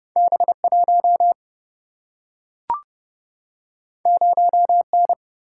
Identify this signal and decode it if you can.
Morse code, heard as a keyed tone, about 30 wpm, 700 Hz. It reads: B1 0N
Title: Single-Letter Single-Number 30wpm